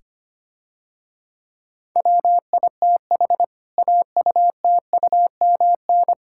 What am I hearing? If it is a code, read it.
Morse code, heard as a keyed tone, about 25 wpm, 700 Hz. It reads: WITH AUTUMN
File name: practice / 25wpm / Difficult Sentences with Hints - 25wpm